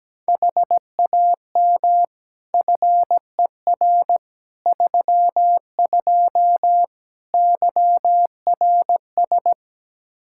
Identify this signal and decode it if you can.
Morse code, heard as a keyed tone, about 17 wpm, 700 Hz. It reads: HAM FER 32 YRS
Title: Practice QSO - Exchange 3 - More Info - 17wpm